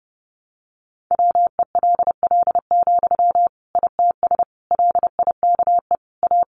Morse code, heard as a keyed tone, about 28 wpm, 700 Hz. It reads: WELL, ITS LIKE A
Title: Sentences from Top 600 Words 30wpm